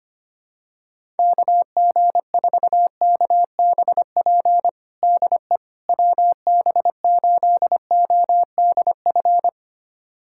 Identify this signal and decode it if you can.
Morse code, heard as a keyed tone, about 25 wpm, 700 Hz. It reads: KG4KBP DE WB8ODF
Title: USA Callsign DE Callsign 25wpm